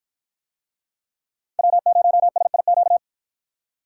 Morse code, heard as a keyed tone, about 51 wpm, 700 Hz. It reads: W0SIQ